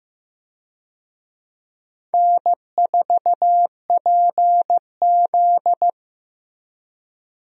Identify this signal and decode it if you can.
Morse code, heard as a keyed tone, about 15 wpm, 700 Hz. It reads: N4PZ
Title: USA Call Signs - 4 Characters - Repeated 2x - 15wpm